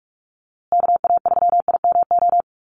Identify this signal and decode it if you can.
Morse code, heard as a keyed tone, about 44 wpm, 700 Hz. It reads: KA3IMO
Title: USA Call Signs Repeated 3x 45wpm